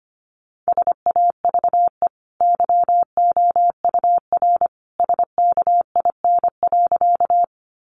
Morse code, heard as a keyed tone, about 25 wpm, 700 Hz. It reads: SAVE YOUR SKIN.
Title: Sentences from Top 1300 Words 25wpm